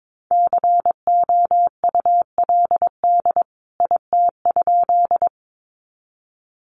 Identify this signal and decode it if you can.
Morse code, heard as a keyed tone, about 22 wpm, 700 Hz. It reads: COULD IT?
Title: A Sound Of Thunder 22wpm